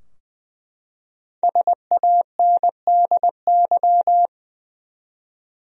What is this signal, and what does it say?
Morse code, heard as a keyed tone, about 20 wpm, 700 Hz. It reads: SANDY
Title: USA Common Names - 20wpm